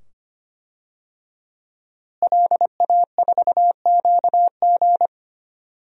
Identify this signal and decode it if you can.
Morse code, heard as a keyed tone, about 25 wpm, 700 Hz. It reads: LA4QG
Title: DXCC-100 Call Signs - 5 Characters - Repeated 2x - 25wpm